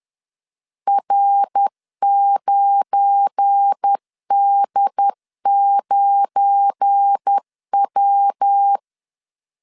Morse code, heard as a keyed tone, about 10 wpm, 800 Hz. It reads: R9D9W
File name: LEZIONI / 7 GF9Z / 7 GF9 10wpm 2747